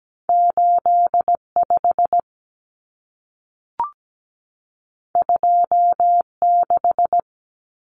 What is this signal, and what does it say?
Morse code, heard as a keyed tone, about 17 wpm, 700 Hz. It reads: 85 26
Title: Numbers - 2-Digit 17wpm